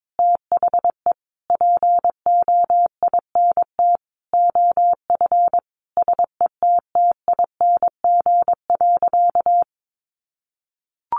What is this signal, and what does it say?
Morse code, heard as a keyed tone, about 22 wpm, 700 Hz. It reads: THE POINT OF SETTING.
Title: Sentences from Top 1300 Words 22wpm